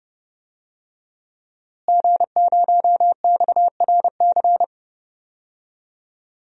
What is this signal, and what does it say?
Morse code, heard as a keyed tone, about 32 wpm, 700 Hz. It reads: G0XRC
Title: DXCC-100 Call Signs - 5 Characters - Repeated 2x - 30wpm